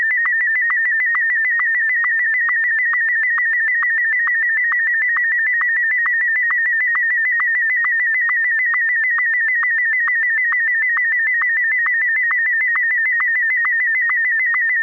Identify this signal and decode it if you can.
SSTV (slow-scan television), Martin 1.